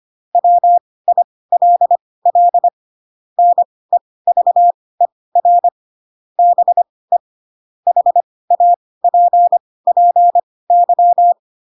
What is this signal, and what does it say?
Morse code, heard as a keyed tone, about 25 wpm, spaced as for 18 wpm, 700 Hz. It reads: WILL NEVER BE HAPPY
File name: Pinocchio / Pinocchio ch4 12 25 ik1hge